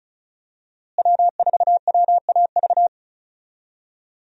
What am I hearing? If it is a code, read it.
Morse code, heard as a keyed tone, about 34 wpm, 700 Hz. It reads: W4WAV
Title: USA Call Signs - 5 Characters - Repeated 2x - 35wpm